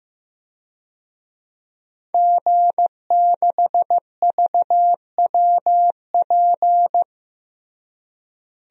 Morse code, heard as a keyed tone, about 14 wpm, 700 Hz. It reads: G6VWP
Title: DXCC-100 Call Signs - 5 Characters - Repeated 2x - 15wpm